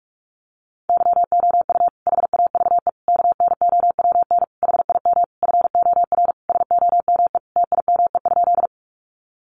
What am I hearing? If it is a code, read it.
Morse code, heard as a keyed tone, about 44 wpm, 700 Hz. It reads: YOU HAVE KNOWN HIM FOR SOME TIME?
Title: Sentences from Top 600 Words 45wpm